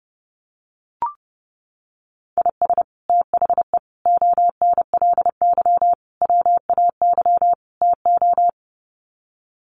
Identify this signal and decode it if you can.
Morse code, heard as a keyed tone, about 30 wpm, 700 Hz. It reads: IS THE ONLY WAY TO
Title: Sets of 5 Words Encore 30wpm